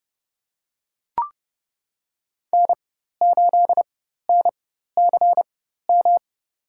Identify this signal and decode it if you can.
Morse code, heard as a keyed tone, about 30 wpm, spaced as for 14 wpm, 700 Hz. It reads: N8NCM
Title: USA Call Signs - 5 Characters - 4x Character Spacing - 30wpm